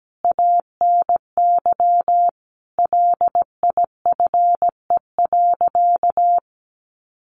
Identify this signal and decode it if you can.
Morse code, heard as a keyed tone, about 17 wpm, 700 Hz. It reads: ANY LIFE.
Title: Sentences from Top 300 Words 17wpm